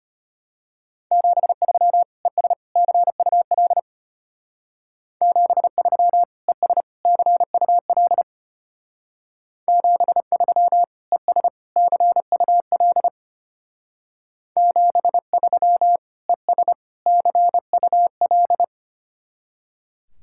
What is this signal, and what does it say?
Morse code, heard as a keyed tone, about 32 wpm, 700 Hz. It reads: 73 ES CUL 73 ES CUL 73 ES CUL 73 ES CUL